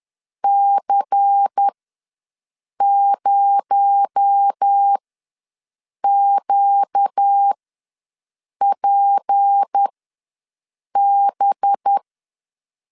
Morse code, heard as a keyed tone, about 10 wpm, spaced as for 6 wpm, 800 Hz. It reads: C0QPB
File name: LEZIONI / 6 PKX0 / 6 PKX0 ws 10wpm 4220